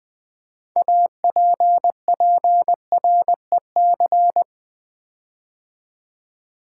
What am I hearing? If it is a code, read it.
Morse code, heard as a keyed tone, about 20 wpm, 700 Hz. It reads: APPREC
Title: Build A Word - 6 Letters At A Time - 12-Letter Words - 20wpm